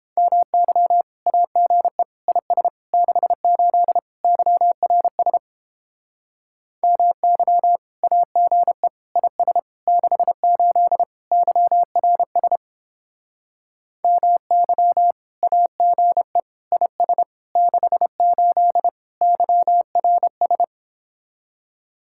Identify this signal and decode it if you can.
Morse code, heard as a keyed tone, about 31 wpm, 700 Hz. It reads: MY AGE IS 68 YRS MY AGE IS 68 YRS MY AGE IS 68 YRS